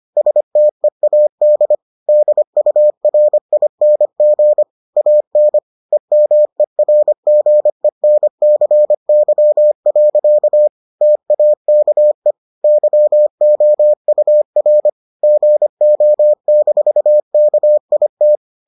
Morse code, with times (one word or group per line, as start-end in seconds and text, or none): 0.2-1.7 STEAD
2.1-4.6 DURING
5.0-5.6 AN
5.9-10.7 EMERGENCY.
11.0-12.3 TAKE
12.6-14.9 YOUR
15.2-18.4 GO-KIT